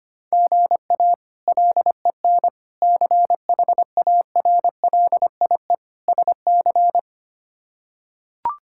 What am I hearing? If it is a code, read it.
Morse code, heard as a keyed tone, about 25 wpm, 700 Hz. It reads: GA LEN CHARLIE SC
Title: K1USN SST QSO Practice - 25wpm